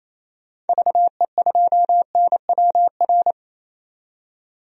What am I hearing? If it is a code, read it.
Morse code, heard as a keyed tone, about 28 wpm, 700 Hz. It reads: VE2NWR
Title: DXCC-100 Call Signs - 6 Characters - 28wpm